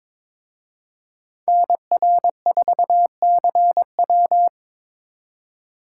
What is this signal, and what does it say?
Morse code, heard as a keyed tone, about 26 wpm, 700 Hz. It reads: NR4CW
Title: Speed Racing - USA 2x2 Call Signs 22wpm